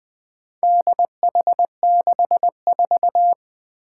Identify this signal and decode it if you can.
Morse code, heard as a keyed tone, about 20 wpm, 700 Hz. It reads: DH64
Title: Four Challenging Letter-Number Combinations 20wpm